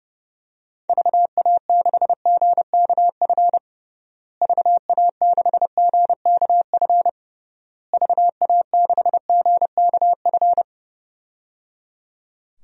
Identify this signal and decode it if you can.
Morse code, heard as a keyed tone, about 30 wpm, 700 Hz. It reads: VA6GKF VA6GKF VA6GKF
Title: DXCC-100 Call Signs - 6 Characters - Repeated 3x - 30wpm